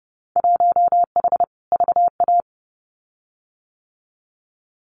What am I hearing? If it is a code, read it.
Morse code, heard as a keyed tone, about 30 wpm, 700 Hz. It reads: 1H VA
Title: Winter Field Day Exchanges 30wpm